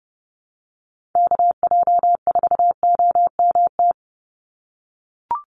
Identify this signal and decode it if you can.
Morse code, heard as a keyed tone, about 30 wpm, 700 Hz. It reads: KJ4OMT
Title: USA Call Signs 30wpm